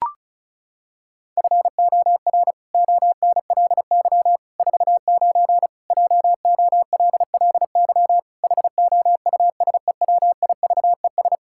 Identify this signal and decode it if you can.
Morse code, heard as a keyed tone, about 36 wpm, 700 Hz. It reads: FOR ONLY 49 JOLLY HOUSEWIVES
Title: Pangram Challenge v2 - 35wpm